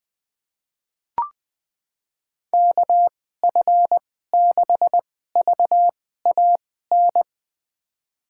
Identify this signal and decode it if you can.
Morse code, heard as a keyed tone, about 20 wpm, spaced as for 14 wpm, 700 Hz. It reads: KF6VAN
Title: USA Call Signs - 2x Character Spacing - 20wpm